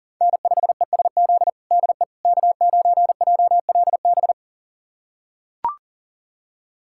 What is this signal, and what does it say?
Morse code, heard as a keyed tone, about 40 wpm, 700 Hz. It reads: N5ESZ DE K9JLB
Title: USA Callsign DE Callsign - 5 Characters - 40wpm